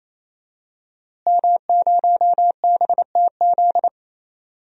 Morse code, heard as a keyed tone, about 28 wpm, 700 Hz. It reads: M0BTZ
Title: DXCC-100 Call Signs - 5 Characters - Repeated 3x - 28wpm